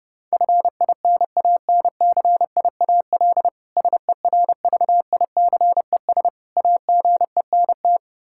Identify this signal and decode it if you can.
Morse code, heard as a keyed tone, about 30 wpm, 700 Hz. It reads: FINANCIAL SERVICES AGENT
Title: Practice QSO - Exchange 3 - More Info - 30wpm